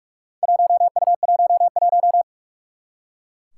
Morse code, heard as a keyed tone, about 44 wpm, 700 Hz. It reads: 1U11
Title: Four Challenging Letter-Number Combinations 45wpm